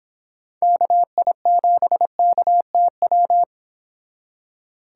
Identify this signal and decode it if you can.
Morse code, heard as a keyed tone, about 31 wpm, 700 Hz. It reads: KI7KTW